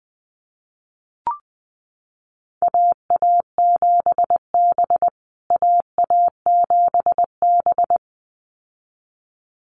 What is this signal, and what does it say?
Morse code, heard as a keyed tone, about 20 wpm, 700 Hz. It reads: AA7B AA7B
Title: USA Call Signs Repeated 2x 20wpm